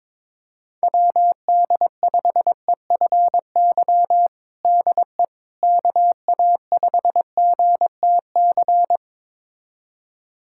Morse code, heard as a keyed tone, about 22 wpm, 700 Hz. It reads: WD5EFY DE KA5GTC
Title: USA Callsign DE Callsign - 6 Characters - 22wpm